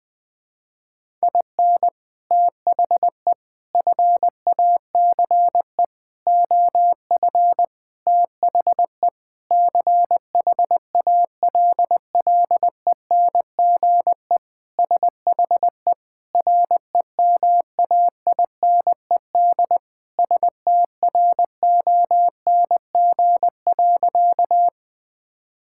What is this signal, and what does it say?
Morse code, heard as a keyed tone, about 20 wpm, 700 Hz. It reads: IN THE FACE OF THE CHALLENGE SHE REMAINED STRONG.
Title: Sentences with 5-Word Phrases - 20wpm